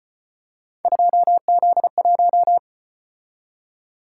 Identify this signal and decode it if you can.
Morse code, heard as a keyed tone, about 34 wpm, 700 Hz. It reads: 2Z1